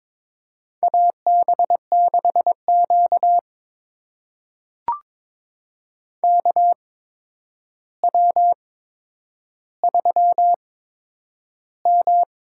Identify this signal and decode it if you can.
Morse code, heard as a keyed tone, about 21 wpm, spaced as for 9 wpm, 700 Hz. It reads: AB6Q KW3M
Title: USA Call Signs - 4 Characters - 8x Character Spacing - 22wpm